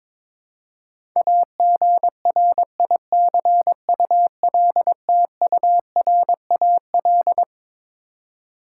Morse code, heard as a keyed tone, about 22 wpm, 700 Hz. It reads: AGRICULTURAL
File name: Twelve Letter Words - 22wpm